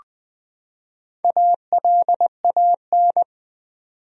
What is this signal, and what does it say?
Morse code, heard as a keyed tone, about 21 wpm, 700 Hz. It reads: ALAN
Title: USA Common Names - Max 4 Letters - 20wpm